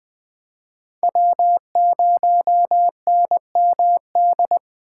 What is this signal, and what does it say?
Morse code, heard as a keyed tone, about 20 wpm, 700 Hz. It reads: W0NMD